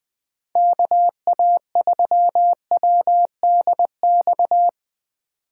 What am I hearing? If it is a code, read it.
Morse code, heard as a keyed tone, about 20 wpm, 700 Hz. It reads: KA3WDX